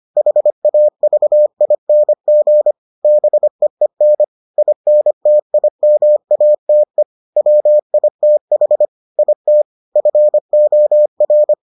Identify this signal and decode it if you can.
Morse code, heard as a keyed tone, about 25 wpm, 600 Hz. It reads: HAVING BEEN INTIMATE WITH IT FOR